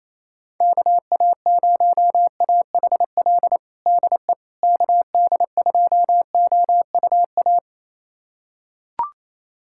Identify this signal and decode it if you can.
Morse code, heard as a keyed tone, about 27 wpm, 700 Hz. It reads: KA0AHL DE KD2OUA
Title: USA Callsign DE Callsign 28wpm